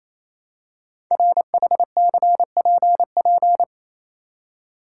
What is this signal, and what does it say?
Morse code, heard as a keyed tone, about 27 wpm, 700 Hz. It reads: RHCPP
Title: Code Groups - 5 Letters - 28wpm